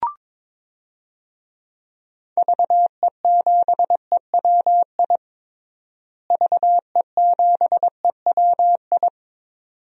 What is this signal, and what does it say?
Morse code, heard as a keyed tone, about 22 wpm, 700 Hz. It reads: VE7EWI VE7EWI
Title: DXCC-100 Call Signs - 6 Characters - Repeated 2x - 22wpm